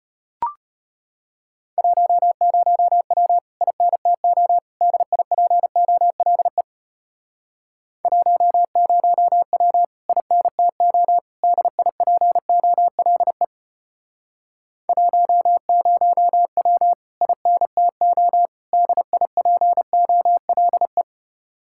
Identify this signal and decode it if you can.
Morse code, heard as a keyed tone, about 33 wpm, 700 Hz. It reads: 10W INTO DIPOLE 10W INTO DIPOLE 10W INTO DIPOLE